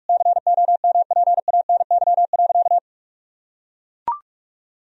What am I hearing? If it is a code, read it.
Morse code, heard as a keyed tone, about 41 wpm, 700 Hz. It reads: COMPANY.
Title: Sentences from Top 1000 Words - Review Part 3 - 45wpm